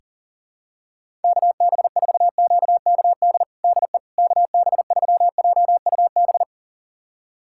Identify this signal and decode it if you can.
Morse code, heard as a keyed tone, about 40 wpm, 700 Hz. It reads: KB4QKD DE KB3JUB
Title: USA Callsign DE Callsign - 6 Characters - 40wpm